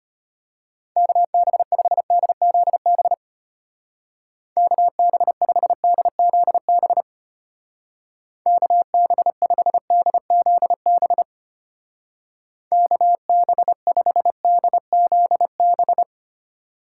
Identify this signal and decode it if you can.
Morse code, heard as a keyed tone, about 32 wpm, 700 Hz. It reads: KB5DZB KB5DZB KB5DZB KB5DZB